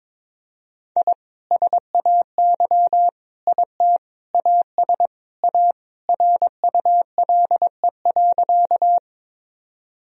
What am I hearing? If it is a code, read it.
Morse code, heard as a keyed tone, about 22 wpm, 700 Hz. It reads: I SAY IT AS A RULE.
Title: Sentences from Top 1000 Words - Review Part 2 - 22wpm